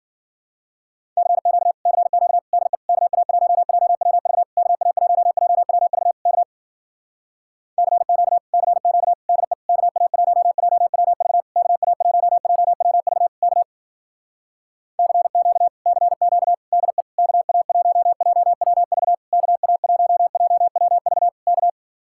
Morse code, heard as a keyed tone, about 51 wpm, 700 Hz. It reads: CQ CQ DE KA1JWV KA1JWV K CQ CQ DE KA1JWV KA1JWV K CQ CQ DE KA1JWV KA1JWV K